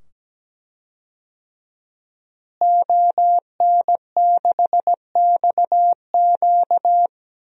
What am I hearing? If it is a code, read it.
Morse code, heard as a keyed tone, about 17 wpm, 700 Hz. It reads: ON6XQ
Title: DXCC-100 Call Signs - 5 Characters - Repeated 2x - 17wpm